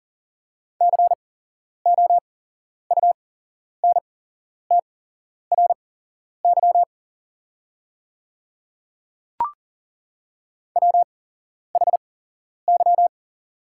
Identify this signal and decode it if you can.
Morse code, heard as a keyed tone, about 39 wpm, spaced as for 11 wpm, 700 Hz. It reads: COUNTRY WHY
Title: 8x Character Spacing - Top 300 Words - 40wpm